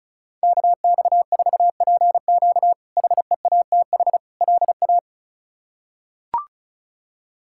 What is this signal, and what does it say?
Morse code, heard as a keyed tone, about 36 wpm, 700 Hz. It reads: KX4PQ HEATH LA
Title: K1USN SST QSO Practice - 35wpm